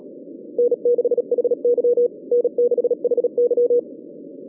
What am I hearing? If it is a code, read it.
Morse code, heard as a keyed tone, about 37 wpm, 480 Hz. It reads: N6HY N6HY